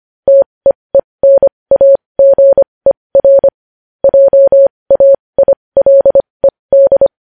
Morse code, heard as a keyed tone, about 25 wpm, spaced as for 20 wpm, 550 Hz. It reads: TEENAGER JAILED
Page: news headlines in morse code at 20 wpm